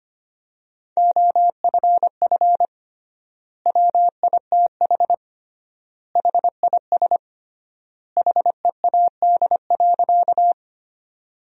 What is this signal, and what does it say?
Morse code, heard as a keyed tone, about 25 wpm, 700 Hz. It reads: OFF WITH HIS HEAD.
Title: 3x Word-Spacing - Sentences from Top 300 Words - 25wpm